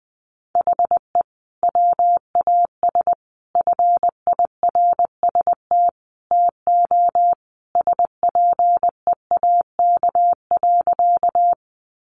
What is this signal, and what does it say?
Morse code, heard as a keyed tone, about 20 wpm, 700 Hz. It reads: HE WAS FIRST TO SPEAK.
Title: Sentences from Top 1100 Words 20wpm